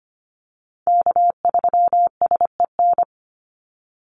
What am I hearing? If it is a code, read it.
Morse code, heard as a keyed tone, about 25 wpm, 700 Hz. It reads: K3SEN
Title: Call Signs - 25wpm